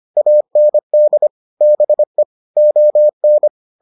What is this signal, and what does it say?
Morse code, heard as a keyed tone, about 25 wpm, 600 Hz. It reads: AND BE ON